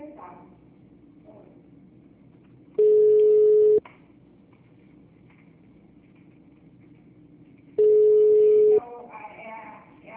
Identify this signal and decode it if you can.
Ringback tone (Europe, 425 Hz), with speech.